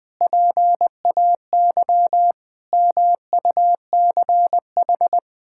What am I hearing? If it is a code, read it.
Morse code, heard as a keyed tone, about 20 wpm, 700 Hz. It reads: PAY MUCH
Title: Sentences from Top 1000 Words - Review Part 6 - 20wpm